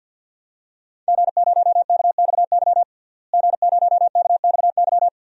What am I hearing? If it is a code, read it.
Morse code, heard as a keyed tone, about 53 wpm, 700 Hz. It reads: G0KXY G0KXY
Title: DXCC-100 Call Signs - 5 Characters - Repeated 2x - 50wpm